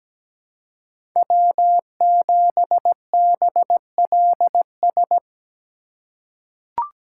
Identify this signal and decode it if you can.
Morse code, heard as a keyed tone, about 17 wpm, 700 Hz. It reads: W7BLS
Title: USA Call Signs - 5 Characters - 3x Character Spacing - 17wpm